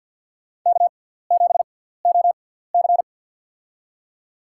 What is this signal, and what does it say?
Morse code, heard as a keyed tone, about 50 wpm, spaced as for 17 wpm, 700 Hz. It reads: K7OC
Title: USA Call Signs - 4 Characters - 6x Character Spacing - 50wpm